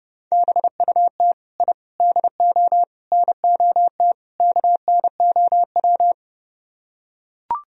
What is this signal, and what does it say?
Morse code, heard as a keyed tone, about 30 wpm, 700 Hz. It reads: BUT I DO NOT KNOW